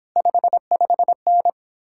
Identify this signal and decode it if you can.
Morse code, heard as a keyed tone, about 29 wpm, 700 Hz. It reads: 55N